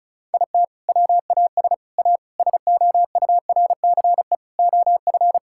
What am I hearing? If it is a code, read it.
Morse code, heard as a keyed tone, about 36 wpm, 700 Hz. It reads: IT WAS A SOURCE OF
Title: Sentences from Top 1000 Words - Review Part 5 - 35wpm